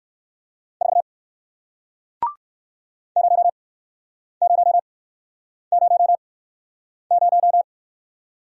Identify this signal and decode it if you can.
Morse code, heard as a keyed tone, about 58 wpm, spaced as for 9 wpm, 700 Hz. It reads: 4 0000